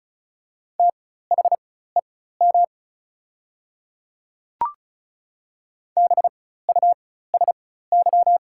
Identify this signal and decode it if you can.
Morse code, heard as a keyed tone, about 36 wpm, spaced as for 16 wpm, 690 Hz. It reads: THEM BUSY
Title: 4x Character Spacing - Top 500 Words - 35wpm